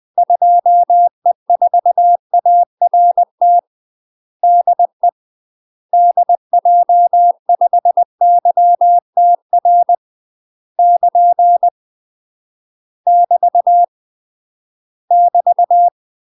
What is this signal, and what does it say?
Morse code, heard as a keyed tone, about 20 wpm, 700 Hz. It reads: 2E4ART DE DJ5YTR <KN> <BT> <BT>
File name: UK MASTER QSOs TRAINING / 20WPM / UK MASTER QSOs 0063